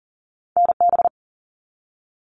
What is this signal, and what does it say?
Morse code, heard as a keyed tone, about 38 wpm, 700 Hz. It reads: NB
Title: Canadian Provinces and Territories 40wpm